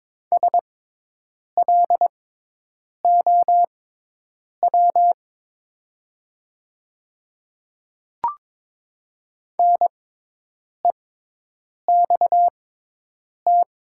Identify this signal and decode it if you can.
Morse code, heard as a keyed tone, about 22 wpm, spaced as for 8 wpm, 700 Hz. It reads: SLOW NEXT
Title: 6x Character Spacing - Top 500 Words - 22wpm